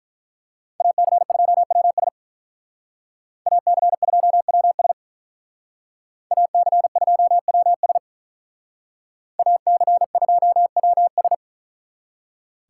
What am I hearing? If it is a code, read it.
Morse code, heard as a keyed tone, about 43 wpm, 700 Hz. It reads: AC2WS AC2WS AC2WS AC2WS
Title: Speed Racing - USA 2x2 Call Signs 35wpm